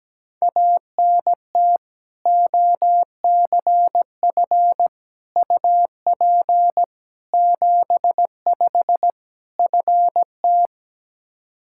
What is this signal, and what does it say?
Morse code, heard as a keyed tone, about 17 wpm, 700 Hz. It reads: ANT OCF UP 75 FT
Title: Practice QSO - Exchange 2 - Info - 17wpm